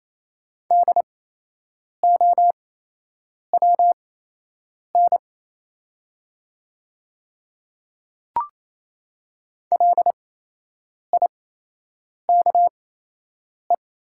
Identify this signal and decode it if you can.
Morse code, heard as a keyed tone, about 28 wpm, spaced as for 8 wpm, 700 Hz. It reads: DOWN LIKE
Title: 8x Character Spacing - Top 200 Words - 28wpm